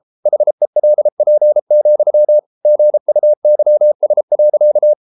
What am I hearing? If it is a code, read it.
Morse code, heard as a keyed tone, about 33 wpm, 600 Hz. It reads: HELP, GUYS.